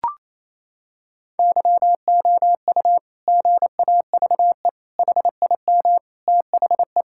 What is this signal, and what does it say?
Morse code, heard as a keyed tone, about 27 wpm, 700 Hz. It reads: YOU GAVE HIM THE
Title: Sentences from Top 1000 Words - Review Part 6 - 28wpm